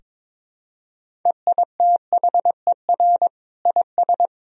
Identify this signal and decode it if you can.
Morse code, heard as a keyed tone, about 22 wpm, 700 Hz. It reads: EITHER IS
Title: Sentences from Top 1000 Words - Review Part 4 - 22wpm